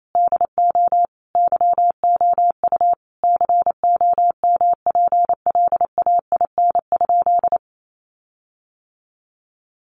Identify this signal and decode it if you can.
Morse code, heard as a keyed tone, about 28 wpm, 700 Hz. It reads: DO YOU COMPLAIN?
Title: Sentences from Top 1300 Words 28wpm